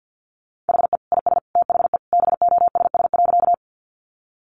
Morse code, heard as a keyed tone, about 50 wpm, 700 Hz. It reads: HE IS THE BOSS.
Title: Sentences from Top 700 Words 50wpm